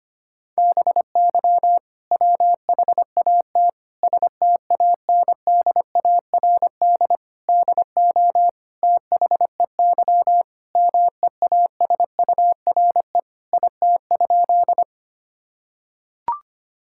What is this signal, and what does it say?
Morse code, heard as a keyed tone, about 25 wpm, 700 Hz. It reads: BY WHAT STANDARD DO THEY MEASURE IT?
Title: Sentences from Top 1000 Words - Review Part 6 - 25wpm